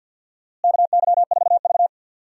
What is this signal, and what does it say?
Morse code, heard as a keyed tone, about 50 wpm, 700 Hz. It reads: KY4V
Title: USA Call Signs - 4 Characters - Repeated 2x - 50wpm